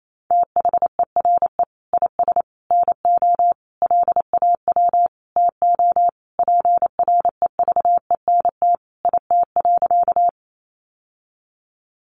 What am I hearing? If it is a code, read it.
Morse code, heard as a keyed tone, about 28 wpm, 700 Hz. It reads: THERE IS NO LAW TO PREVENT IT.
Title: Sentences from Top 1100 Words 28wpm